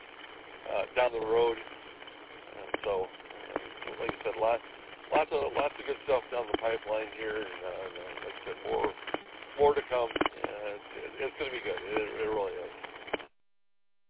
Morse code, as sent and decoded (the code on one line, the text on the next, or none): none